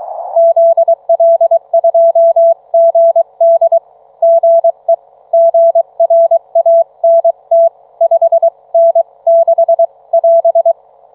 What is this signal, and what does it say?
Morse code, heard as a keyed tone, about 23 wpm, 660 Hz. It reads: ZL2GD GE GRANT 5N6 <AS>